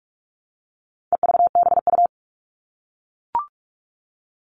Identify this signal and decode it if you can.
Morse code, heard as a keyed tone, about 44 wpm, 700 Hz. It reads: EVBU
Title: All Cut Numbers - 4-Digit 45wpm